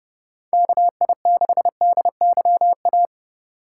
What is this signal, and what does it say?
Morse code, heard as a keyed tone, about 31 wpm, 700 Hz. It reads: KI6DYA
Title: USA Call Signs - 6 Characters - 8x Character Spacing - 30wpm